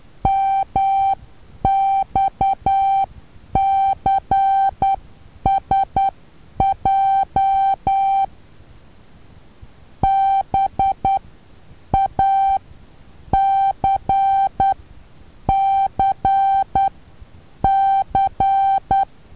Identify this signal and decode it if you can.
Morse code, heard as a keyed tone, about 10 wpm, spaced as for 8 wpm, 780 Hz. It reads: MXCSJ BACCC
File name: Taviro HA6VA B